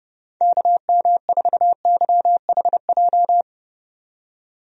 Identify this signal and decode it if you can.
Morse code, heard as a keyed tone, about 32 wpm, 700 Hz. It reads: KM4YHJ
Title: USA Call Signs - 6 Characters - 30wpm